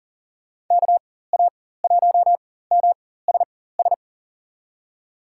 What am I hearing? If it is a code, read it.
Morse code, heard as a keyed tone, about 40 wpm, spaced as for 19 wpm, 700 Hz. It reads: KA1MSS
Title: USA Call Signs - 4x Character Spacing - 40wpm